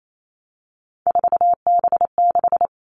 Morse code, heard as a keyed tone, about 28 wpm, 700 Hz. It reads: 4B6
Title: All Cut Numbers - 3-Digit 28wpm